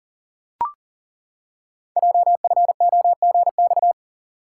Morse code, heard as a keyed tone, about 41 wpm, 700 Hz. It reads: JFOGX
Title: Code Groups - 5 Letters - 40wpm